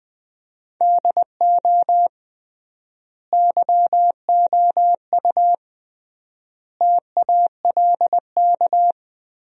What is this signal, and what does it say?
Morse code, heard as a keyed tone, about 20 wpm, 700 Hz. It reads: DO YOU TALK
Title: 3x Word-Spacing - Sentences from Top 500 Words - 20wpm